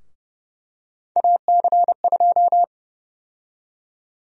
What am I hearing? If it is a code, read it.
Morse code, heard as a keyed tone, about 32 wpm, 700 Hz. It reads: AC2
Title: USA Call Sign Prefixes - 30wpm